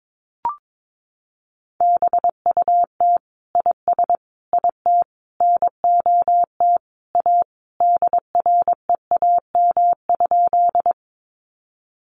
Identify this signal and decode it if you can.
Morse code, heard as a keyed tone, about 22 wpm, 700 Hz. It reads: BUT IS IT NOT A DREAM?
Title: Sentences from Top 1500 Words 22wpm